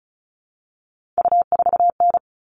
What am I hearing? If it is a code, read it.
Morse code, heard as a keyed tone, about 34 wpm, 700 Hz. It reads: U4N
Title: All Cut Numbers - 3-Digit 35wpm